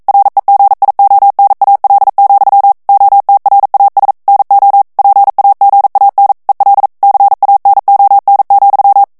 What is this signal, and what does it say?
Morse code, heard as a keyed tone, about 42 wpm, 800 Hz. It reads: REGIONAL, OTRAS NO PAGAN EL CANON,
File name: morse mp3 / ESPAGNOL TEXTE EN CLAIR 1 VITESSE 40 MOTS MINUTES